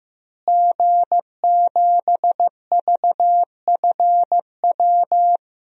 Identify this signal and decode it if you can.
Morse code, heard as a keyed tone, about 15 wpm, 700 Hz. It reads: G7VFW
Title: DXCC-100 Call Signs - 5 Characters - Repeated 3x - 15wpm